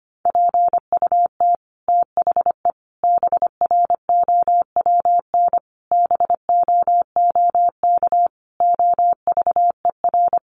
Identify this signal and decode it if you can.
Morse code, heard as a keyed tone, about 25 wpm, 700 Hz. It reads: PUT THE BROWN BOOK OVER
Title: Sentences from Top 1200 Words 25wpm